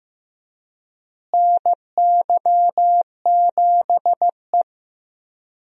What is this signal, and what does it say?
Morse code, heard as a keyed tone, about 15 wpm, 700 Hz. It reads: NY7E
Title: USA Call Signs - 4 Characters - Repeated 2x - 15wpm